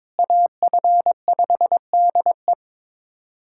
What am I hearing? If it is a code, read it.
Morse code, heard as a keyed tone, about 22 wpm, 700 Hz. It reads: AF5DE
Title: USA Callsign DE Callsign - 5 Characters - 22wpm